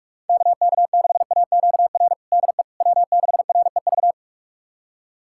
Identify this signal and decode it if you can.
Morse code, heard as a keyed tone, about 43 wpm, 700 Hz. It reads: KK6AQR DE W6REV
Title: USA Callsign DE Callsign 45wpm